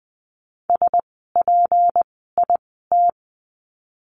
Morse code, heard as a keyed tone, about 20 wpm, spaced as for 14 wpm, 700 Hz. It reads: SPIT
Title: 153 Lesson 25f - Letter p - Words 20wpm